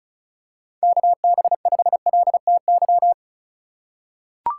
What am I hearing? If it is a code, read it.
Morse code, heard as a keyed tone, about 36 wpm, 700 Hz. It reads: KB5LTY
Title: USA Call Signs - 6 Characters - 4x Character Spacing - 35wpm